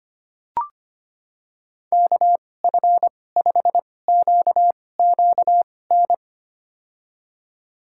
Morse code, heard as a keyed tone, about 25 wpm, spaced as for 18 wpm, 700 Hz. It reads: KF5QQN
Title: USA Call Signs - 6 Characters - 2x Character Spacing - 25wpm